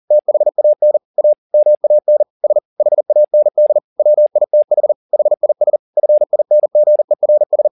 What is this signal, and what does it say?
Morse code, heard as a keyed tone, about 40 wpm, 600 Hz. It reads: THAN A MAN S HAND WITH HIS FINGERS
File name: the old man and the sea 2 40wpm